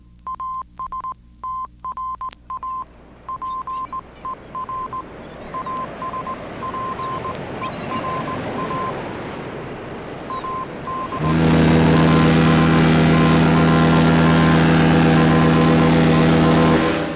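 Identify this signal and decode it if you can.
Morse code, heard as a keyed tone, about 18 wpm, 1040 Hz, with noise. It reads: ASTRA PER ASPERA AD ASTRA PER
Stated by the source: Morse code from ships, on the Voyager disc.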